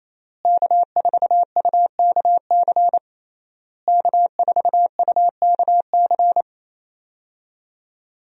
Morse code, heard as a keyed tone, about 28 wpm, 700 Hz. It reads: K4UKC K4UKC
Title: USA Call Signs - 5 Characters - Repeated 3x - 28wpm